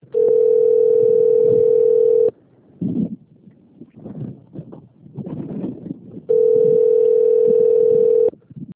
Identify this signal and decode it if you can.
Ringback tone (North America), 440 + 480 Hz.
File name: rg-1600-unknown- - - .1629